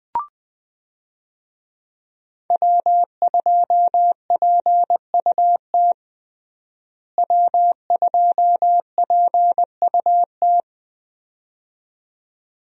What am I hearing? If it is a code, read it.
Morse code, heard as a keyed tone, about 20 wpm, 700 Hz. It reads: W2PUT W2PUT
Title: USA Call Signs - 5 Characters - Repeated 2x - 20wpm